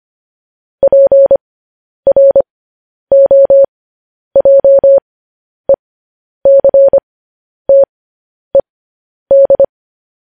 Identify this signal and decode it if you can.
Morse code, heard as a keyed tone, about 25 wpm, spaced as for 10 wpm, 550 Hz. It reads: PROJECTED
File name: podcast / News headlines-1x-25x10